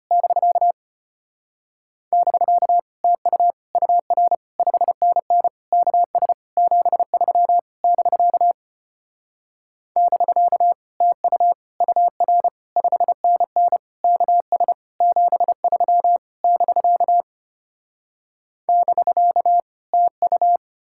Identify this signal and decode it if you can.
Morse code, heard as a keyed tone, about 33 wpm, 700 Hz. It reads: <BK> <BK> TU UR 5NN KS 73 <BK> <BK> TU UR 5NN KS 73 <BK> <BK> TU